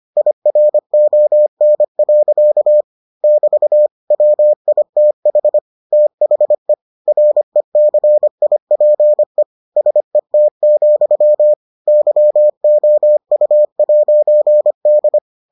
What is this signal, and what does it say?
Morse code, heard as a keyed tone, about 25 wpm, 600 Hz. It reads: IRON. <BT> WITH THE RECIPE SET, YOU'D